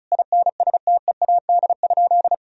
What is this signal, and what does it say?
Morse code, heard as a keyed tone, about 36 wpm, 700 Hz. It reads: INSTEAD?
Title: Sentences from Top 1000 Words - Review Part 4 - 35wpm